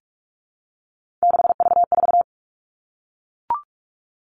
Morse code, heard as a keyed tone, about 44 wpm, 700 Hz. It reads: 6V4
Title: All Cut Numbers - 3-Digit 45wpm